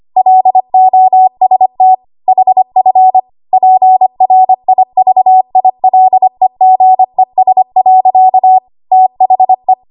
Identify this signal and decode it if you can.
Morse code, heard as a keyed tone, about 25 wpm, 750 Hz. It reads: LOST HF PRIVILEGES. THE
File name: Archive / 25 WPM / 25WPM